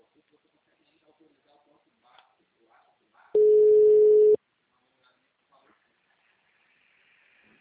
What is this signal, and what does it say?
Ringback tone (Europe, 425 Hz), with speech.